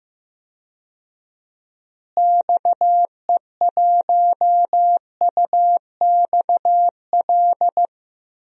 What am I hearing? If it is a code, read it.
Morse code, heard as a keyed tone, about 15 wpm, 700 Hz. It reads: XE1UXL
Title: DXCC-100 Call Signs - 6 Characters - Repeated 2x - 15wpm